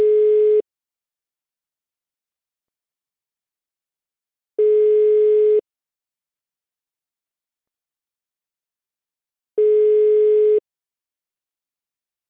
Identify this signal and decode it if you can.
Ringback tone (Europe, 425 Hz).
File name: Freizeichen DE